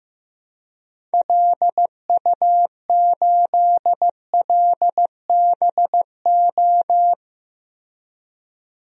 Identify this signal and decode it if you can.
Morse code, heard as a keyed tone, about 16 wpm, 700 Hz. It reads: LU8LBO
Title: DXCC-100 Call Signs - 6 Characters - Repeated 2x - 15wpm